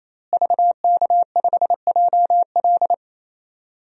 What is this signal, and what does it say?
Morse code, heard as a keyed tone, about 28 wpm, 700 Hz. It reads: VK5JL
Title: DXCC-100 Call Signs - 5 Characters - Repeated 3x - 28wpm